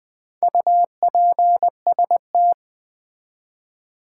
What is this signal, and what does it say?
Morse code, heard as a keyed tone, about 20 wpm, 700 Hz. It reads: UPST
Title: Build A Word - 2 Letters At A Time - 8-Letter Words - 20wpm